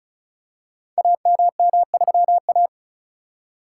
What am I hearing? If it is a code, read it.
Morse code, heard as a keyed tone, about 36 wpm, 700 Hz. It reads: AMM3A
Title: Code Groups - 5 Letter:Numbers - 35wpm